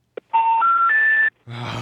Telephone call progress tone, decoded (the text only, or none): special information tone